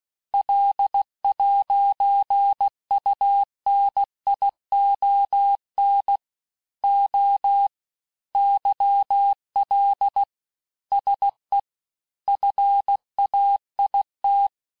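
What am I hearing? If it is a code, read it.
Morse code, heard as a keyed tone, about 16 wpm, 800 Hz. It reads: L'UNION O YL SE FAIT
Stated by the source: Morse course text, end of QSO, sent at 15 words per minute.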